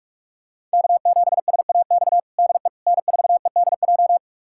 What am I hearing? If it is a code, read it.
Morse code, heard as a keyed tone, about 45 wpm, 700 Hz. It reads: K7SAX DE N4EDJ
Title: USA Callsign DE Callsign - 5 Characters - 45wpm